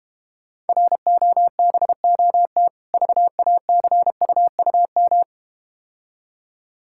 Morse code, heard as a keyed tone, about 31 wpm, 700 Hz. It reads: ROBOT VACUUM
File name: Difficult Words - Speed-Racing with Hints - 25wpm